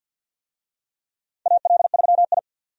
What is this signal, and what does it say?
Morse code, heard as a keyed tone, about 53 wpm, 700 Hz. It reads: AL3I
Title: USA Call Signs - 4 Characters - 50wpm